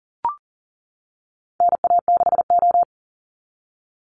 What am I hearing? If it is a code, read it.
Morse code, heard as a keyed tone, about 40 wpm, 700 Hz. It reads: NA6O
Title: NA SOTA Chaser Callsigns 40wpm